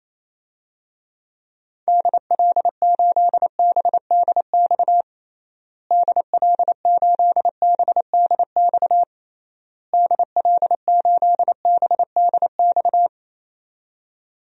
Morse code, heard as a keyed tone, about 27 wpm, 700 Hz. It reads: DL8BDX DL8BDX DL8BDX
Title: DXCC-100 Call Signs - 6 Characters - Repeated 3x - 28wpm